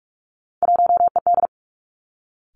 Morse code, heard as a keyed tone, about 46 wpm, 700 Hz. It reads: 1ED
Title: Three Challenging Letter-Number Combinations 45wpm